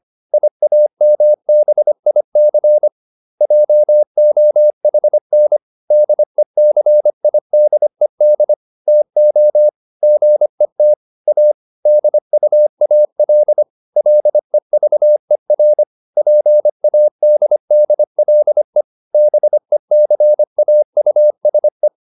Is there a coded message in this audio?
Yes: Morse code from 0.3 to 21.9 s, heard as a keyed tone, about 25 wpm, 600 Hz. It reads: IAMBIC JOHN DECIDED TO GET A DUAL LEVER PADDLE BECAUSE